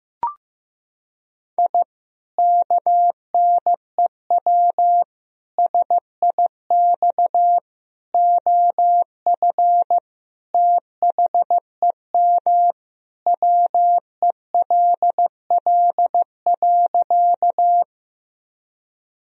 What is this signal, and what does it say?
Morse code, heard as a keyed tone, about 15 wpm, 700 Hz. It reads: I KNEW SIX OF THEM WELL.
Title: Sentences from Top 1000 Words - Review Part 2 - 15wpm